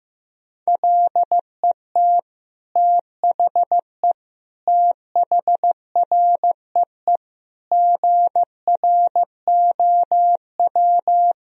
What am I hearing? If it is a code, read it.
Morse code, heard as a keyed tone, about 16 wpm, 700 Hz. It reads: LET THE THREE GROW
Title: Sentences from Top 1000 Words - Review Part 1 - 15wpm